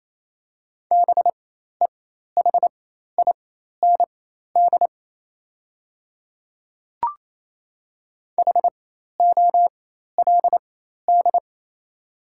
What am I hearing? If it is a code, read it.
Morse code, heard as a keyed tone, about 28 wpm, spaced as for 13 wpm, 690 Hz. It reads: BEHIND HOLD
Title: 4x Character Spacing - Top 500 Words - 28wpm